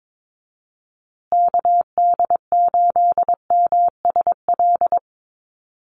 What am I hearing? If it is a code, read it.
Morse code, heard as a keyed tone, about 22 wpm, 700 Hz. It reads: KD8MSL
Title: USA Call Signs Repeated 3x 22wpm